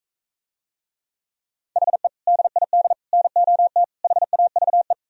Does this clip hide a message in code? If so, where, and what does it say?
Yes: Morse code from 1.8 to 4.9 s, heard as a keyed tone, about 45 wpm, 700 Hz. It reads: HE DID NOT HAVE